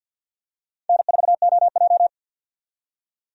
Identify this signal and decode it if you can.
Morse code, heard as a keyed tone, about 52 wpm, 700 Hz. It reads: N4OJ
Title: USA Call Signs - 4 Characters - 50wpm